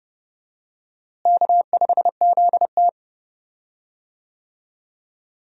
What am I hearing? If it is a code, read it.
Morse code, heard as a keyed tone, about 32 wpm, 700 Hz. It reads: K5ZT
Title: USA Call Signs - 4 Characters - Repeated 2x - 30wpm